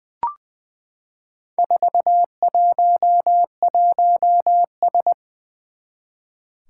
Morse code, heard as a keyed tone, about 20 wpm, 700 Hz. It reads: 411S
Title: Four Challenging Letter-Number Combinations 20wpm